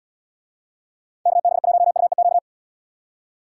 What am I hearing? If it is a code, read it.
Morse code, heard as a keyed tone, about 76 wpm, 700 Hz. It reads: KB1UEQ